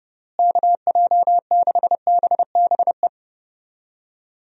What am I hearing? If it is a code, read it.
Morse code, heard as a keyed tone, about 31 wpm, 700 Hz. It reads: KJ6BBE
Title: USA Call Signs - 6 Characters - Repeated 3x - 30wpm